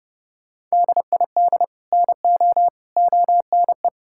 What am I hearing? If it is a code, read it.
Morse code, heard as a keyed tone, about 30 wpm, 700 Hz. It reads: DID NO ONE
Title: Sentences from Top 1600 Words 30wpm